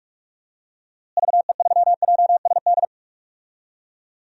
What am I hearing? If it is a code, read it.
Morse code, heard as a keyed tone, about 41 wpm, 700 Hz. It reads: VE3JSD
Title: DXCC-100 Call Signs - 6 Characters - Repeated 3x - 45wpm